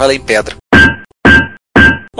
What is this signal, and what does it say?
Speech. There is no code.